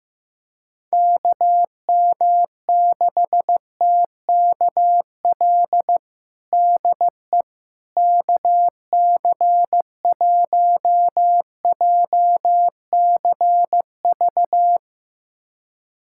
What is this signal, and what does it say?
Morse code, heard as a keyed tone, about 15 wpm, 700 Hz. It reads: KM6TKL DE KC1JCV
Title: USA Callsign DE Callsign 15wpm